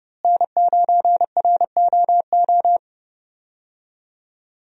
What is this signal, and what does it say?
Morse code, heard as a keyed tone, about 32 wpm, 700 Hz. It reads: N9ROO
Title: USA Call Signs - 5 Characters - Repeated 3x - 30wpm